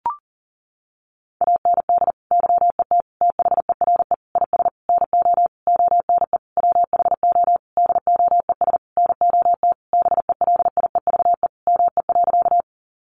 Morse code, heard as a keyed tone, about 40 wpm, 700 Hz. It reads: AND YET THERE IS NO ONE WHO DOES NOT BELIEVE ME.
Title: Sentences from Top 1000 Words 40wpm